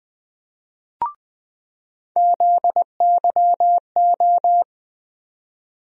Morse code, heard as a keyed tone, about 20 wpm, 700 Hz. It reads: ZYO
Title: USA Call Sign Postfixes - 3 Letters 20wpm